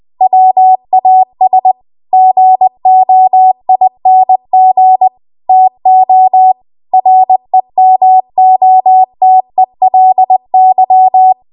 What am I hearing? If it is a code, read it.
Morse code, heard as a keyed tone, about 20 wpm, 750 Hz. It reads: WAS GOING TO REMOTELY
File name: Morse / 20WPM